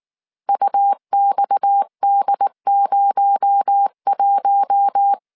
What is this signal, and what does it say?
Morse code, heard as a keyed tone, about 19 wpm, 800 Hz. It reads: UXD01
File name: CW / DET / 1 6 1500C S C 18wpm 2356